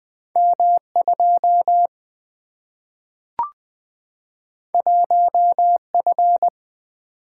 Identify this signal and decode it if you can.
Morse code, heard as a keyed tone, about 20 wpm, 700 Hz. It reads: M2 1F
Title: Single-Letter Single-Number 20wpm